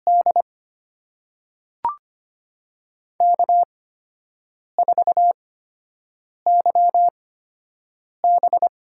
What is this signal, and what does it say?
Morse code, heard as a keyed tone, about 25 wpm, spaced as for 7 wpm, 700 Hz. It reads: D K4YB
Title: USA Call Signs - 4 Characters - 8x Character Spacing - 25wpm